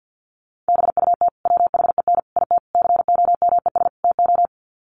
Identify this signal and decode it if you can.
Morse code, heard as a keyed tone, about 50 wpm, 700 Hz. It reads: BUT WHEN IT COMES TO
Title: Sets of 5 Words 50wpm